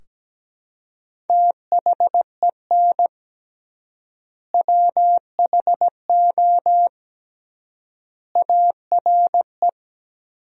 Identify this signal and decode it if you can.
Morse code, heard as a keyed tone, about 17 wpm, 700 Hz. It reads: THEN WHO ARE
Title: 3x Word-Spacing - Sentences from Top 100 Words - 17wpm